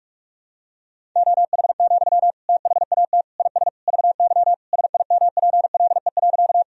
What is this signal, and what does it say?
Morse code, heard as a keyed tone, about 46 wpm, 700 Hz. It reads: OH, THAT IS VY SIMPLE.
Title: Sentences from Top 500 Words and Abbreviations 45wpm